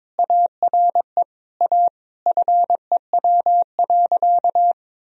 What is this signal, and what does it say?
Morse code, heard as a keyed tone, about 22 wpm, 700 Hz. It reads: ARE A FEW.
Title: Sentences from Top 1000 Words - Review Part 1 - 22wpm